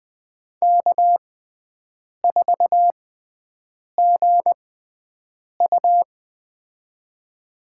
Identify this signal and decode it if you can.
Morse code, heard as a keyed tone, about 21 wpm, spaced as for 7 wpm, 700 Hz. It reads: K4GU
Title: USA Call Signs - 6x Character Spacing - 20wpm